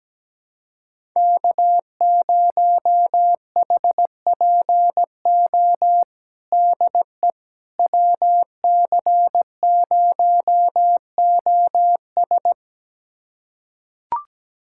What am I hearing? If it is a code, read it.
Morse code, heard as a keyed tone, about 17 wpm, 700 Hz. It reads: K0HPO DE WC0OS
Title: USA Callsign DE Callsign - 5 Characters - 17wpm